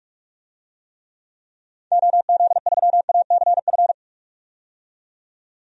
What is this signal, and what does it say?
Morse code, heard as a keyed tone, about 43 wpm, 700 Hz. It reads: OZ3ACF
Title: DXCC-100 Call Signs - 6 Characters - 45wpm